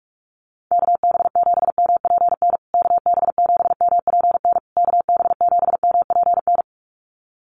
Morse code, heard as a keyed tone, about 44 wpm, 700 Hz. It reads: KB7MPN KB7MPN KB7MPN
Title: USA Call Signs Repeated 3x 45wpm